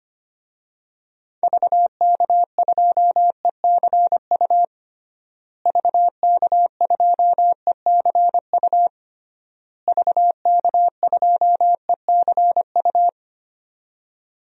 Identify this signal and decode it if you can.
Morse code, heard as a keyed tone, about 25 wpm, 700 Hz. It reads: VK2ECU VK2ECU VK2ECU
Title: DXCC-100 Call Signs - 6 Characters - Repeated 3x - 25wpm